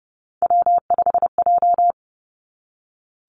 Morse code, heard as a keyed tone, about 30 wpm, 700 Hz. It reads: W5J
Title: Three Challenging Letter-Number Combinations 30wpm